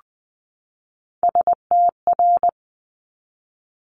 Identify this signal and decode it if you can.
Morse code, heard as a keyed tone, about 20 wpm, 700 Hz. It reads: STR
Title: Multi-letter Phonemes 20wpm